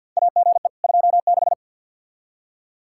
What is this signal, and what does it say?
Morse code, heard as a keyed tone, about 48 wpm, 700 Hz. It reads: AGE 26
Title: Practice QSO - Exchange 3 - More Info - 50wpm